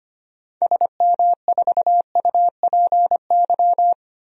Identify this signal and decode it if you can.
Morse code, heard as a keyed tone, about 25 wpm, 700 Hz. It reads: SM4UPY
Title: DXCC-100 Call Signs - 6 Characters - Repeated 2x - 25wpm